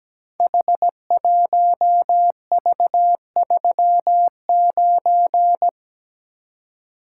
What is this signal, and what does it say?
Morse code, heard as a keyed tone, about 17 wpm, 700 Hz. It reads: H1V39
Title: Code Groups - 5 Letter:Numbers - 17wpm